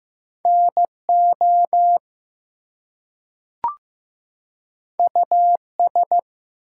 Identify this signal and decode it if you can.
Morse code, heard as a keyed tone, about 15 wpm, 700 Hz. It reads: NO US
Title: Two Letter Words - 15wpm